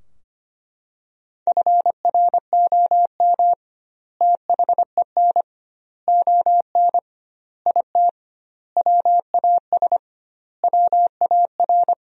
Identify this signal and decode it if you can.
Morse code, heard as a keyed tone, about 25 wpm, 700 Hz. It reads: FROM THEN ON IT WAS WAR
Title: 2x Word-Spacing - Sentences from Top 400 Words - 25wpm